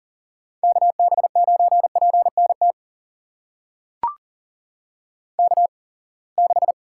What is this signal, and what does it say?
Morse code, heard as a keyed tone, about 38 wpm, spaced as for 27 wpm, 700 Hz. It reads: KB9PNT K6
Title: USA Call Signs - 8x Character Spacing - 40wpm